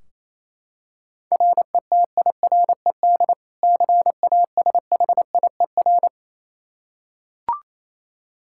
Morse code, heard as a keyed tone, about 28 wpm, 700 Hz. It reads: RETIRED CASHIER
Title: Practice QSO - Exchange 3 - More Info - 28wpm